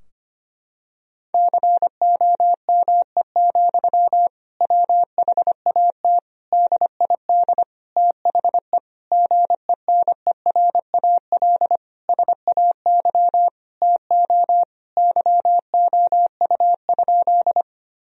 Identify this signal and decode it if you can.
Morse code, heard as a keyed tone, about 25 wpm, 700 Hz. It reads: COME, WHAT DID THE GENERAL SAY TO YOU?
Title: Sentences from Top 1000 Words - Review Part 3 - 25wpm